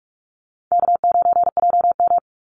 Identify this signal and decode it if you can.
Morse code, heard as a keyed tone, about 46 wpm, 700 Hz. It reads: K9JM
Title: Easy Call Signs Repeated 3x - 45wpm